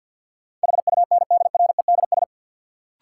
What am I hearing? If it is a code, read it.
Morse code, heard as a keyed tone, about 53 wpm, 700 Hz. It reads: HUNDREDS
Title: Sets of 4 Words - New Word Drill - 50wpm